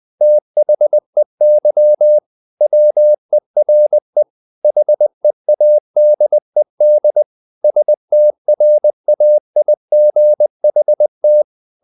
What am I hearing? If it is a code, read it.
Morse code, heard as a keyed tone, about 20 wpm, 600 Hz. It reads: THEY WERE HEADED STRAIGHT
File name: the old man and the sea part 2 20wpm